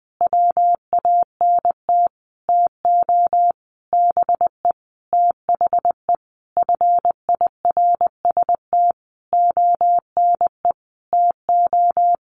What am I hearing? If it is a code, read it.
Morse code, heard as a keyed tone, about 20 wpm, 700 Hz. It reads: WANT TO BE THE FIRST ONE TO
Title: Sentences from Top 1300 Words 20wpm